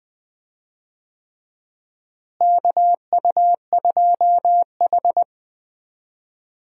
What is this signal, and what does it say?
Morse code, heard as a keyed tone, about 21 wpm, 700 Hz. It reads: KU2H
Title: USA Call Signs - 4 Characters - Repeated 3x - 20wpm